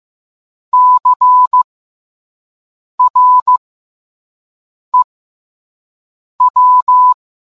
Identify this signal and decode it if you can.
Morse code, heard as a keyed tone, about 15 wpm, spaced as for 5 wpm, 1000 Hz. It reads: CREW